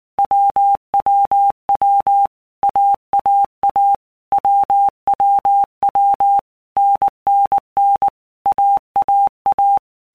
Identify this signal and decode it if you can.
Morse code, heard as a keyed tone, about 19 wpm, 800 Hz. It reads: WWW AAA WWW NNN AAA
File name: Basic Letter Groups By Blocks / Groups@18 / AWUNDG@18